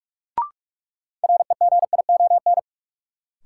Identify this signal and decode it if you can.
Morse code, heard as a keyed tone, about 42 wpm, 700 Hz. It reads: REGION
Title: Sentences from Top 1500 Words - New Word Drill - 45wpm